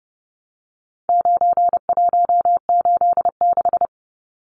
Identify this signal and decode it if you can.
Morse code, heard as a keyed tone, about 30 wpm, 700 Hz. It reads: 9186
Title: Numbers - 4-Digit 30wpm